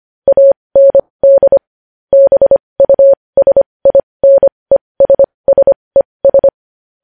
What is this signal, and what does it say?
Morse code, heard as a keyed tone, about 25 wpm, spaced as for 20 wpm, 550 Hz. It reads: AND BUSINESSES
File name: podcast / News headlines-1x-25x20